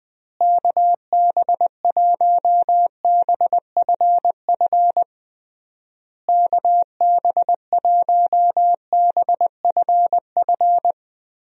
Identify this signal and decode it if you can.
Morse code, heard as a keyed tone, about 21 wpm, 700 Hz. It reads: KB1BFF KB1BFF
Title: USA Call Signs - 6 Characters - Repeated 3x - 20wpm